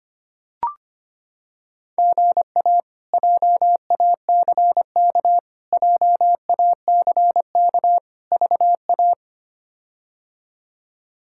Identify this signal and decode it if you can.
Morse code, heard as a keyed tone, about 25 wpm, 700 Hz. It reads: GA JACK JACK VA
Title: K1USN SST QSO Practice - 25wpm